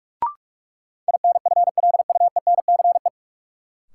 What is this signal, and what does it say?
Morse code, heard as a keyed tone, about 43 wpm, 700 Hz. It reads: INFLUENCE
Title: Sentences from Top 1100 Words - New Word Drill - 45wpm